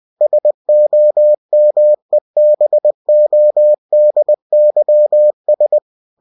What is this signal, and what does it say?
Morse code, heard as a keyed tone, about 20 wpm, 600 Hz. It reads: SOMEBODYS
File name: cw-books / winnie-the-pooh / winnie-20wpm-page-0002